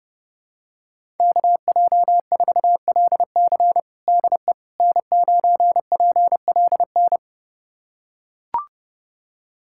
Morse code, heard as a keyed tone, about 30 wpm, 700 Hz. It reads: KJ4LC DE N9PLN
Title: USA Callsign DE Callsign - 5 Characters - 30wpm